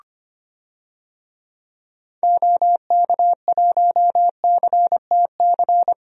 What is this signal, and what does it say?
Morse code, heard as a keyed tone, about 25 wpm, 700 Hz. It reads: OK1CTC
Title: DXCC-100 Call Signs - 6 Characters - 25wpm